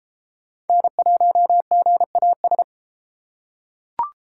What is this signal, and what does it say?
Morse code, heard as a keyed tone, about 31 wpm, 700 Hz. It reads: N1GAS